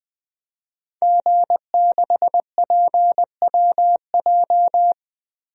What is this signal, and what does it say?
Morse code, heard as a keyed tone, about 20 wpm, 700 Hz. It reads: G6PWJ
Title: DXCC-100 Call Signs - 5 Characters - Repeated 3x - 20wpm